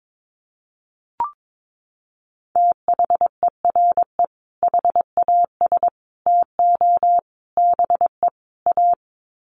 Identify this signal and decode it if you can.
Morse code, heard as a keyed tone, about 22 wpm, 700 Hz. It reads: THERE HAS TO BE A
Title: Sets of 5 Words Encore 22wpm